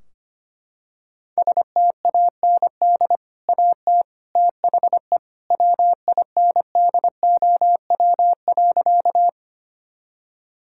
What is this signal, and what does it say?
Morse code, heard as a keyed tone, about 25 wpm, 700 Hz. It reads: STAND AT THE WINDOW.
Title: Sentences from Top 1600 Words 25wpm